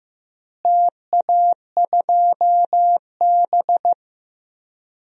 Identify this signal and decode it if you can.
Morse code, heard as a keyed tone, about 15 wpm, 700 Hz. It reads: TA2B
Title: DXCC-100 Call Signs - 4 Characters - Repeated 2x - 15wpm